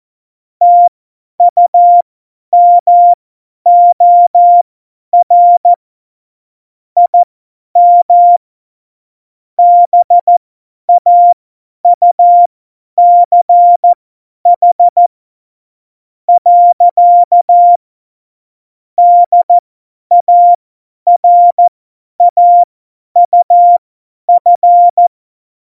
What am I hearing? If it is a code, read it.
Morse code, heard as a keyed tone, about 14 wpm, spaced as for 10 wpm, 700 Hz. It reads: TUMOR IM BAUCH. DARAUF
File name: Morsekurs / Witze / Witz 06. Februar 2018, 50 F70